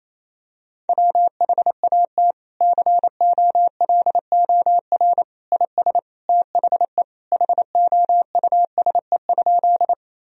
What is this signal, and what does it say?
Morse code, heard as a keyed tone, about 28 wpm, 700 Hz. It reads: WHAT COLOR IS THE HOUSE?
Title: Sentences from Top 1000 Words - Review Part 2 - 28wpm